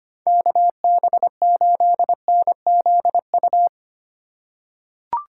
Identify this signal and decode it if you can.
Morse code, heard as a keyed tone, about 25 wpm, 700 Hz. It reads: KB8NZU
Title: USA Call Signs - 6 Characters - 2x Character Spacing - 25wpm